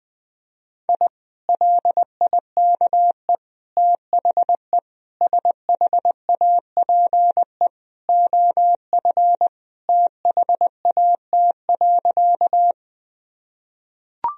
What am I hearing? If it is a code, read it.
Morse code, heard as a keyed tone, about 21 wpm, 700 Hz. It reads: I LIKE THE SHAPE OF THAT.
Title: Sentences from Top 1000 Words - Review Part 3 - 20wpm